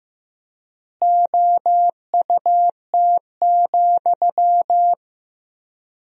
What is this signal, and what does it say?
Morse code, heard as a keyed tone, about 15 wpm, 700 Hz. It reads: OUT,
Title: 4x Word-Spacing - Sentences from Top 500 Words - 15wpm